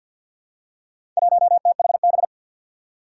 Morse code, heard as a keyed tone, about 47 wpm, 700 Hz. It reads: 1THB
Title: Four Challenging Letter-Number Combinations 50wpm